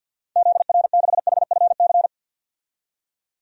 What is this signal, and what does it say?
Morse code, heard as a keyed tone, about 51 wpm, 700 Hz. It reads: ZR6HFC
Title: DXCC-100 Call Signs - 6 Characters - 50wpm